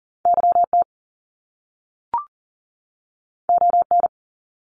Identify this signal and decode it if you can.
Morse code, heard as a keyed tone, about 40 wpm, 700 Hz. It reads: YT ON
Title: Canadian Provinces and Territories 40wpm